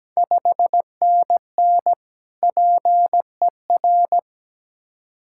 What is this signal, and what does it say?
Morse code, heard as a keyed tone, about 17 wpm, 700 Hz. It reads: 5NN PER
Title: Ontario QSO Party Exchange 17wpm